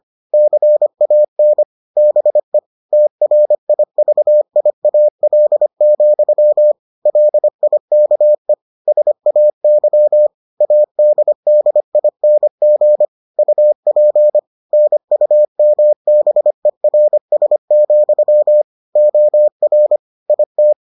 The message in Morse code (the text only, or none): CAN BE TRIVIAL, LIKE SAY ADDING UP NUMBERS, OR IT